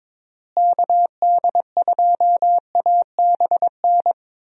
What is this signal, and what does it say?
Morse code, heard as a keyed tone, about 21 wpm, 700 Hz. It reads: KD2ABN
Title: USA Call Signs - 6 Characters - Repeated 2x - 22wpm